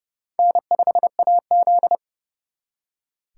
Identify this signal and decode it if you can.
Morse code, heard as a keyed tone, about 36 wpm, 700 Hz. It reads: N5AZ
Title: Speed Racing - USA 1x2 Call Signs 30wpm